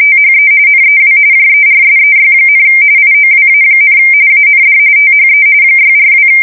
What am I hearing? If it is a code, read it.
RTTY, about 45.45 baud. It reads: THE NEW JERSEY, IOWA, AND THE 642
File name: fsk170 / USS-New-Jersey